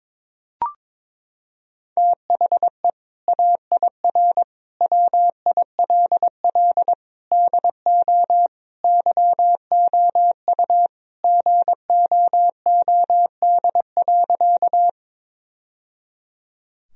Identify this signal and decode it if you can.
Morse code, heard as a keyed tone, about 21 wpm, 700 Hz. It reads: THE AIR WILL DO YOU GOOD.
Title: Sentences from Top 1000 Words - Review Part 1 - 22wpm